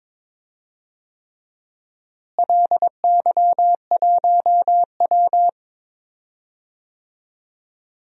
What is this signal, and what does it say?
Morse code, heard as a keyed tone, about 21 wpm, 700 Hz. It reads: LY1W
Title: DXCC-100 Call Signs - 4 Characters - Repeated 3x - 22wpm